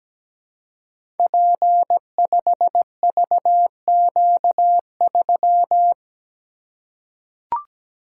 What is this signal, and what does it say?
Morse code, heard as a keyed tone, about 17 wpm, 700 Hz. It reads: P5VQ3
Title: Code Groups - 5 Letter:Numbers - 17wpm